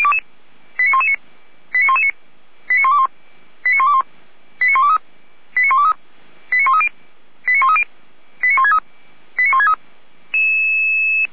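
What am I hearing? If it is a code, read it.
Five-tone selective call (ZVEI), heard as quick runs of five tones, tones of about 70 ms. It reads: 89119 89119 89121 89121 89123 89123 89130 89130 89163 89163
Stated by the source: ZVEI five-tone call 89010 received by SDR, feuerwehr-orlamuende.org fire brigade audio archive.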